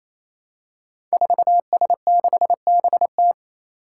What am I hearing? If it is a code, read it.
Morse code, heard as a keyed tone, about 28 wpm, 700 Hz. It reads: 4S6BT
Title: Five Challenging Letter-Number Combinations 28wpm